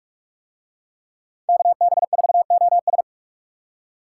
Morse code, heard as a keyed tone, about 44 wpm, 700 Hz. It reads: KB4OS